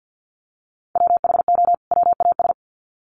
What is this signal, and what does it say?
Morse code, heard as a keyed tone, about 50 wpm, 700 Hz. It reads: WHO WAS
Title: Sets of 2 Words 50wpm